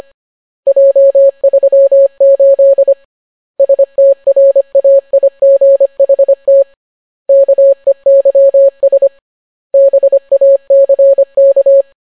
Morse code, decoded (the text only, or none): J38 STRAIGHT KEYS BACK